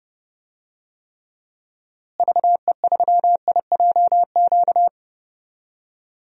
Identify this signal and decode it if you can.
Morse code, heard as a keyed tone, about 30 wpm, 700 Hz. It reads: VE3IJQ
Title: DXCC-100 Call Signs - 6 Characters - Repeated 2x - 30wpm